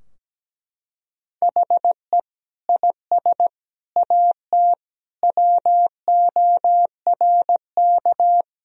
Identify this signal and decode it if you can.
Morse code, heard as a keyed tone, about 17 wpm, 700 Hz. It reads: HE IS AT WORK
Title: Sentences from Top 1000 Words - Review Part 4 - 17wpm